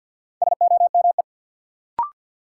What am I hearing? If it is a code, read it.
Morse code, heard as a keyed tone, about 47 wpm, 700 Hz. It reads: SOME
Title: Sets of 2 Words - New Word Drill - 50wpm